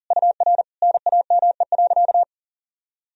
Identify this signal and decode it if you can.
Morse code, heard as a keyed tone, about 40 wpm, 700 Hz. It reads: UR NAME.
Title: Sentences from Top 200 Words and Abbreviations 40wpm